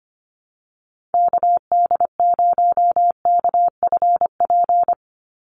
Morse code, heard as a keyed tone, about 25 wpm, 700 Hz. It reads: KD0KFP
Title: USA Call Signs 25wpm